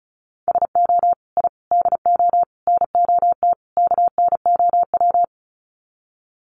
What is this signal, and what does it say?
Morse code, heard as a keyed tone, about 34 wpm, 700 Hz. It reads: SO I DO NOT KNOW
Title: Sets of 5 Words Encore 35wpm